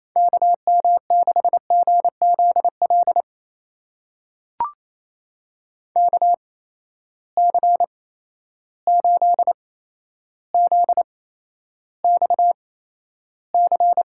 Morse code, heard as a keyed tone, about 28 wpm, spaced as for 12 wpm, 700 Hz. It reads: KM6GZL KC8ZXC
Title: USA Call Signs - 6 Characters - 8x Character Spacing - 28wpm